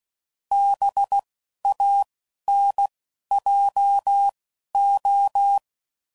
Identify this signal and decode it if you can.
Morse code, heard as a keyed tone, about 16 wpm, spaced as for 11 wpm, 800 Hz. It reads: BANJO